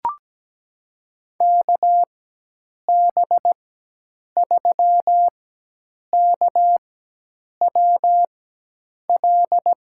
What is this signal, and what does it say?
Morse code, heard as a keyed tone, about 17 wpm, spaced as for 8 wpm, 700 Hz. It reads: KB3KWL
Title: USA Call Signs - 4x Character Spacing - 17wpm